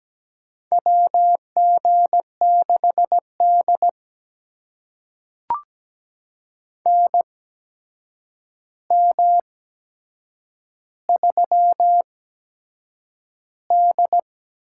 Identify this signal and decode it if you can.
Morse code, heard as a keyed tone, about 17 wpm, spaced as for 7 wpm, 700 Hz. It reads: WG6D NM3D
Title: USA Call Signs - 4 Characters - 8x Character Spacing - 17wpm